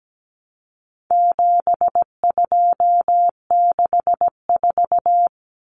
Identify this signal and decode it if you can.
Morse code, heard as a keyed tone, about 17 wpm, 700 Hz. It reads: 7264
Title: Numbers - 4-Digit 17wpm